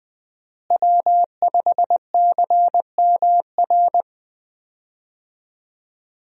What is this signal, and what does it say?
Morse code, heard as a keyed tone, about 20 wpm, 700 Hz. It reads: W5CMR
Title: USA Call Signs - 5 Characters - Repeated 3x - 20wpm